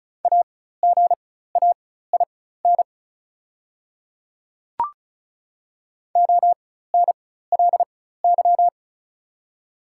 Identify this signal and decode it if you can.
Morse code, heard as a keyed tone, about 34 wpm, spaced as for 16 wpm, 700 Hz. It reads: AGAIN ONLY
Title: 4x Character Spacing - Top 200 Words - 35wpm